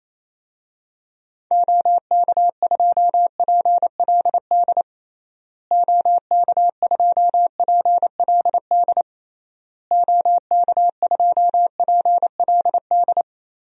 Morse code, heard as a keyed tone, about 27 wpm, 700 Hz. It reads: OK2PLD OK2PLD OK2PLD
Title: DXCC-100 Call Signs - 6 Characters - Repeated 3x - 28wpm